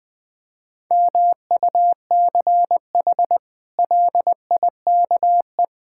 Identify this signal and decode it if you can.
Morse code, heard as a keyed tone, about 20 wpm, 700 Hz. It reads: MUCH LIKE
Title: Sets of 8 Words - 20wpm